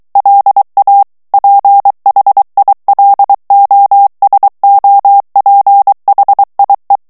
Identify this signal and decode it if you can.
Morse code, heard as a keyed tone, about 23 wpm, 800 Hz. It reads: LA PHILOSOPHIE
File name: FRANCAIS TEXTE CLAIR 2 VITESSE 22 MOTS MINUTES